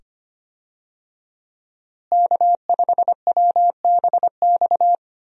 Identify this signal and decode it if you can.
Morse code, heard as a keyed tone, about 25 wpm, 700 Hz. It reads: K5WBX
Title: USA Call Signs - 5 Characters - Repeated 2x - 25wpm